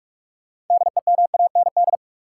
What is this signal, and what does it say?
Morse code, heard as a keyed tone, about 46 wpm, 700 Hz. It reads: DEMAND
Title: Sentences from Top 800 Words - New Word Drill - 45wpm